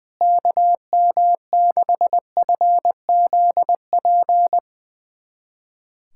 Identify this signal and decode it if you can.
Morse code, heard as a keyed tone, about 21 wpm, 700 Hz. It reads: KM6FZP
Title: USA Callsign DE Callsign - 6 Characters - 20wpm